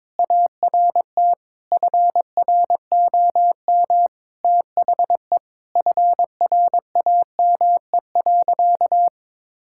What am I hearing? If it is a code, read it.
Morse code, heard as a keyed tone, about 22 wpm, 700 Hz. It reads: ART FROM THE FRAME.
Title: Sentences from Top 1000 Words - Review Part 6 - 22wpm